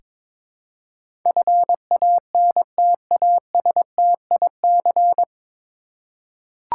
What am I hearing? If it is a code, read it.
Morse code, heard as a keyed tone, about 22 wpm, 700 Hz. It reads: FANTASTIC
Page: Morse Code Ninja practice 22wpm